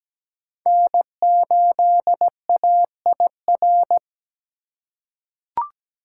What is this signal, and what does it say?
Morse code, heard as a keyed tone, about 17 wpm, 700 Hz. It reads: N8AIR
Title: USA Call Signs - 5 Characters - 2x Character Spacing - 17wpm